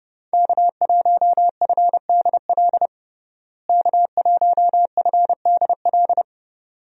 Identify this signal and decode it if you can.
Morse code, heard as a keyed tone, about 30 wpm, 700 Hz. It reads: K1FDL K1FDL
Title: USA Call Signs - 5 Characters - Repeated 2x - 30wpm